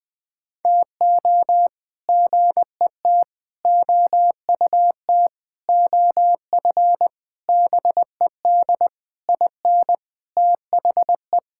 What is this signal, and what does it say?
Morse code, heard as a keyed tone, about 20 wpm, 700 Hz. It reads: TO GET OUT OF BED IN THE
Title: Sentences with 8-Word Phrases - Spoken Beginning - 20wpm